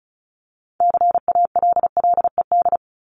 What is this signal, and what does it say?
Morse code, heard as a keyed tone, about 34 wpm, 700 Hz. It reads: CALLED
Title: ED Suffix - Top 100 Words with Base Words 35wpm